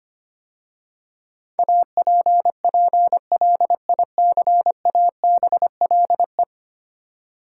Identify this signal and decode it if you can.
Morse code, heard as a keyed tone, about 25 wpm, 700 Hz. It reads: APPLICABLE
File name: 10 Letter Words - First 3 Letters Given - 25wpm